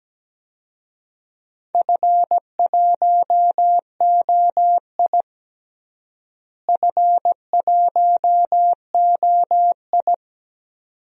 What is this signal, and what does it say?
Morse code, heard as a keyed tone, about 17 wpm, 700 Hz. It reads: F1OI F1OI
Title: DXCC-100 Call Signs - 4 Characters - Repeated 2x - 17wpm